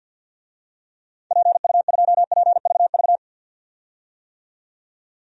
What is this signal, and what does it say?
Morse code, heard as a keyed tone, about 50 wpm, 700 Hz. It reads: PU2PVV
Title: DXCC-100 Call Signs - 6 Characters - 50wpm